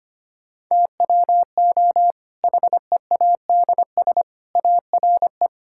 Morse code, heard as a keyed tone, about 25 wpm, 700 Hz. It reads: TWO HEADS ARE
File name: practice / 25wpm / Difficult Sentences with Hints - 25wpm